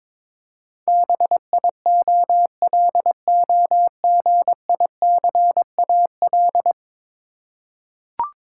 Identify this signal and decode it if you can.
Morse code, heard as a keyed tone, about 22 wpm, 700 Hz. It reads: BIOLOGICAL
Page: morse-code-ninja-practice-22wpm